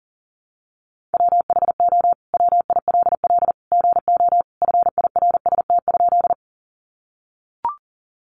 Morse code, heard as a keyed tone, about 40 wpm, 700 Hz. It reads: WHO WILL GO FIRST?
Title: Sentences from Top 100 Words 40wpm